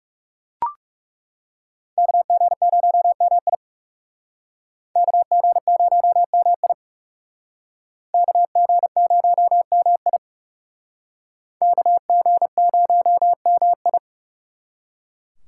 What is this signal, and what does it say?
Morse code, heard as a keyed tone, about 38 wpm, 700 Hz. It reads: KG0MI KG0MI KG0MI KG0MI